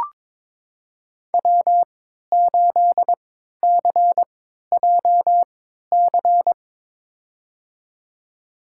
Morse code, heard as a keyed tone, about 22 wpm, spaced as for 13 wpm, 700 Hz. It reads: W8CJC
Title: USA Call Signs - 5 Characters - 3x Character Spacing - 22wpm